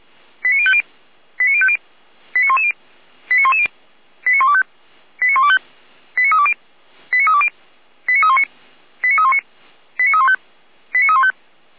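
Five-tone selective call (ZVEI), heard as quick runs of five tones, tones of about 70 ms. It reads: 89050 89050 89110 89110 89125 89125 89320 89320 89319 89319 89315 89315
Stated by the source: ZVEI five-tone call 89187 received by SDR.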